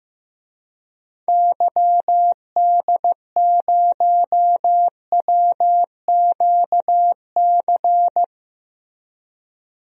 Morse code, heard as a keyed tone, about 15 wpm, 700 Hz. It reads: YD0WQC
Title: DXCC-100 Call Signs - 6 Characters - 15wpm